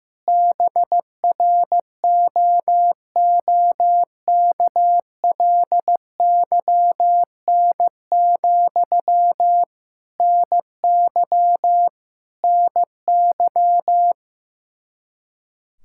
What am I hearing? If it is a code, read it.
Morse code, heard as a keyed tone, about 15 wpm, 700 Hz. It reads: BROOKLYN, NY NY
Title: Practice QSO - Exchange 1 - Beginning - 15wpm